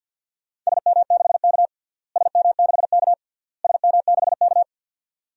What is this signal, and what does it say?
Morse code, heard as a keyed tone, about 47 wpm, 700 Hz. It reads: SM6K SM6K SM6K
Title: DXCC-100 Call Signs - 4 Characters - Repeated 3x - 50wpm